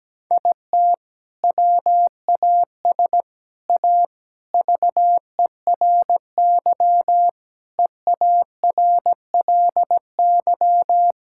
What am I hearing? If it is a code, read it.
Morse code, heard as a keyed tone, about 17 wpm, 700 Hz. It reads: IT WAS A VERY EARLY
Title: Sentences from Top 1000 Words - Review Part 6 - 17wpm